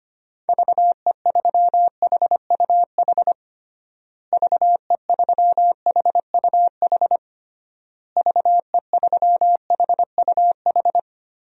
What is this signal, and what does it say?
Morse code, heard as a keyed tone, about 25 wpm, 700 Hz. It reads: VE3HUH VE3HUH VE3HUH
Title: DXCC-100 Call Signs - 6 Characters - Repeated 3x - 25wpm